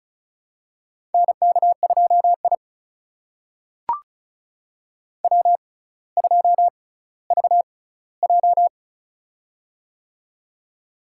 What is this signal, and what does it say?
Morse code, heard as a keyed tone, about 36 wpm, spaced as for 18 wpm, 700 Hz. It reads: NK2I W2VJ
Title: USA Call Signs - 4 Characters - 6x Character Spacing - 35wpm